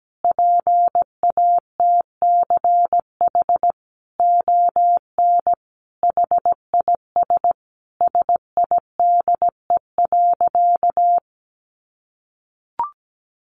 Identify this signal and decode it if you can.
Morse code, heard as a keyed tone, about 17 wpm, 700 Hz. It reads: PATCH ON HIS SIDE.
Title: A Sound Of Thunder 17wpm